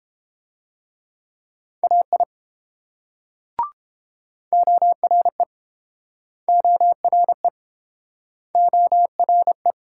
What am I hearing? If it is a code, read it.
Morse code, heard as a keyed tone, about 30 wpm, 700 Hz. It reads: AI ORE ORE ORE